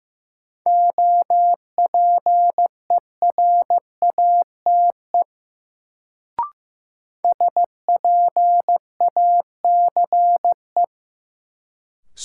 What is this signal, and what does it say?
Morse code, heard as a keyed tone, about 15 wpm, 700 Hz. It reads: OPERATE SPACE
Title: Sentences from Top 1500 Words - New Word Drill Alt - 15wpm